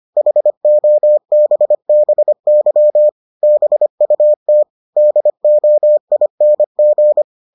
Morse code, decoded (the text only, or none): HOBBY BUT DOING